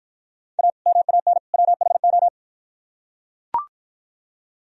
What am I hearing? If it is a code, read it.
Morse code, heard as a keyed tone, about 43 wpm, 700 Hz. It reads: A MAN WHO
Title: Speed-Racing - Sets of 3 Words 35wpm